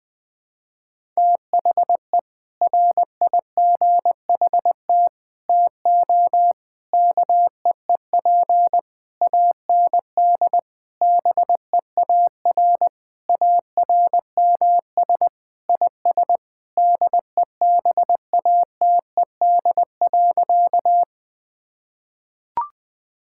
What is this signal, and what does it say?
Morse code, heard as a keyed tone, about 20 wpm, 700 Hz. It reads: THE RIGHT TO KEEP AND BEAR ARMS IS DEBATED.
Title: Sentences with 7-Word Phrases - 20wpm